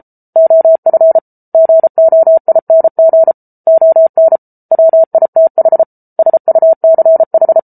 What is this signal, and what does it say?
Morse code, heard as a keyed tone, about 34 wpm, 650 Hz. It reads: OF GOING ON WITH SUCH